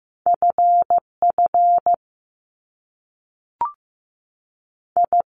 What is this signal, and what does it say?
Morse code, heard as a keyed tone, about 15 wpm, 700 Hz. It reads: F F I
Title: Top Two Letter Combinations in Words 15wpm